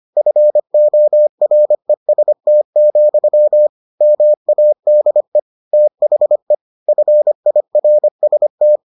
Morse code, heard as a keyed tone, about 25 wpm, 600 Hz. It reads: FOREST, MADE THE FIRST